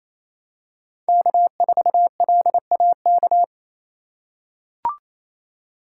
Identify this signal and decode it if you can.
Morse code, heard as a keyed tone, about 28 wpm, 700 Hz. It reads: K4LAK
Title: USA Call Signs - 5 Characters - 2x Character Spacing - 28wpm